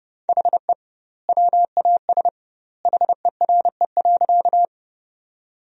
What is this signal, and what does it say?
Morse code, heard as a keyed tone, about 30 wpm, 700 Hz. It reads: HE WAS HERE.
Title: 2x Word-Spacing - Sentences from Top 200 Words - 30wpm